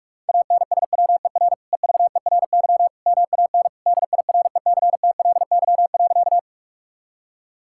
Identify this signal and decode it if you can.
Morse code, heard as a keyed tone, about 47 wpm, 700 Hz. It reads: ANSWER EVERY MAN DIRECTLY.
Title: Sentences from Top 1000 Words - Review Part 5 - 45wpm